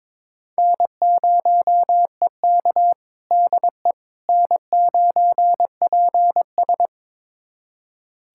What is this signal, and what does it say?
Morse code, heard as a keyed tone, about 22 wpm, 700 Hz. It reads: N0EK DE N9PS
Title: USA Callsign DE Callsign - 4 Characters - 22wpm